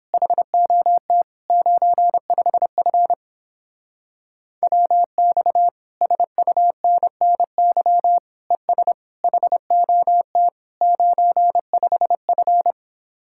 Morse code, heard as a keyed tone, about 29 wpm, 700 Hz. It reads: HOT 95F WX SUNNY ES HOT 95F